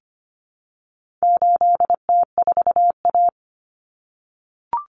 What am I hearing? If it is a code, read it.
Morse code, heard as a keyed tone, about 25 wpm, 700 Hz. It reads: 8T4A
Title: Common Cut Numbers - 4-Digit 25wpm